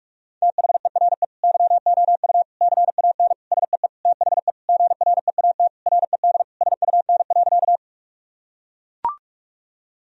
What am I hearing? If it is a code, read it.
Morse code, heard as a keyed tone, about 42 wpm, 700 Hz. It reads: THERE YOU CAN SEE THE GREAT RED SUN.
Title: Sentences from Top 1000 Words - Review Part 2 - 45wpm